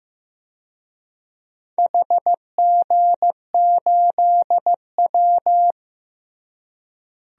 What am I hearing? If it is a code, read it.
Morse code, heard as a keyed tone, about 15 wpm, 700 Hz. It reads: HG8W
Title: DXCC-100 Call Signs - 4 Characters - Repeated 3x - 15wpm